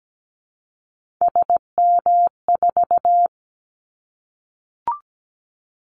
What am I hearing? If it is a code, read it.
Morse code, heard as a keyed tone, about 17 wpm, 700 Hz. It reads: SM4
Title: Callsign Prefixes for Top 25 DXCC - Repeated 3x - 17wpm